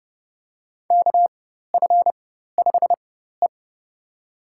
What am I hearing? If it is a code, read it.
Morse code, heard as a keyed tone, about 30 wpm, spaced as for 14 wpm, 700 Hz. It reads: KF5E
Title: USA Call Signs - 4 Characters - 4x Character Spacing - 30wpm